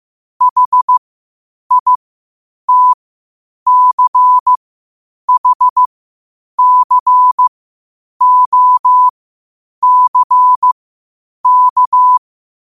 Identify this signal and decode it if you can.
Morse code, heard as a keyed tone, about 15 wpm, spaced as for 8 wpm, 1000 Hz. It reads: HITCHCOCK